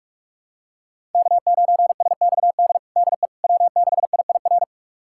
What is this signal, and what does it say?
Morse code, heard as a keyed tone, about 45 wpm, 700 Hz. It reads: K9SXD DE W6IIR
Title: USA Callsign DE Callsign - 5 Characters - 45wpm